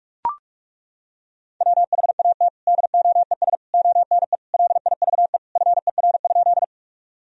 Morse code, heard as a keyed tone, about 42 wpm, 700 Hz. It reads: WHAT DOES ONE LIVE FER?
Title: Sentences from Top 200 Words and Abbreviations 45wpm